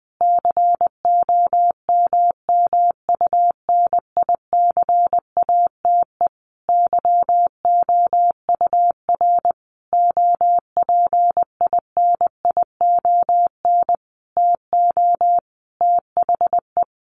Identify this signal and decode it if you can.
Morse code, heard as a keyed tone, about 19 wpm, 700 Hz. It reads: COMMUNICATE YOUR OPINION TO THE
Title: Sentences from Top 1300 Words 20wpm